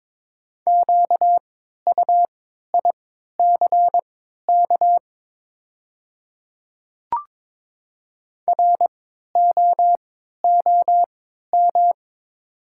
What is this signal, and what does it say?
Morse code, heard as a keyed tone, about 22 wpm, spaced as for 12 wpm, 700 Hz. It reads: QUICK ROOM
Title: 3x Character Spacing - Top 500 Words - 22wpm